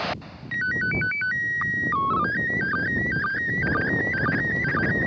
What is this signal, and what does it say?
SSTV (slow-scan television), PD 120.
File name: 08-ARISS-SSTV-Expedition-63-ARISS-Series-16-Satellites